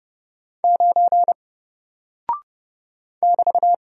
Morse code, heard as a keyed tone, about 30 wpm, 700 Hz. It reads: 9 <BT>
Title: Single Character (Letter:Number:Punctuation:Prosign) 30wpm